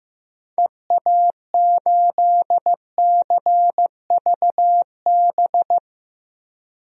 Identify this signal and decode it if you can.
Morse code, heard as a keyed tone, about 15 wpm, 700 Hz. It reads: EA8CVB
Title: DXCC-100 Call Signs - 6 Characters - 15wpm